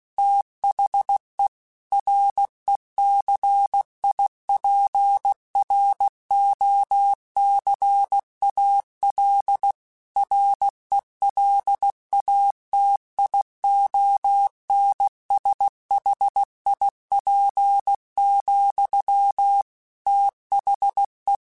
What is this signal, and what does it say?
Morse code, heard as a keyed tone, about 16 wpm, 800 Hz. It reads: THE RECIPROCAL RELATIONSHIP, THE
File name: Technical Subjects / Technical Subjects-mp3s / XC @15